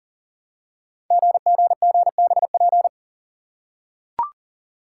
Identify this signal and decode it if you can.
Morse code, heard as a keyed tone, about 38 wpm, 690 Hz. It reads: GGGBP
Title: Code Groups - 5 Letters - 40wpm